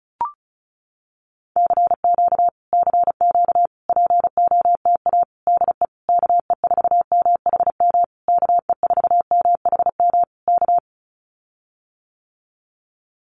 Morse code, heard as a keyed tone, about 34 wpm, 700 Hz. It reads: CQ CQ POTA DE KE4MHM KE4MHM K
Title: Common QSO Phrases Encore 35wpm